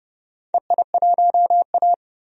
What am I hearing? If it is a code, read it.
Morse code, heard as a keyed tone, about 29 wpm, 700 Hz. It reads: EI1A
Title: DXCC-100 Call Signs - 4 Characters - 30wpm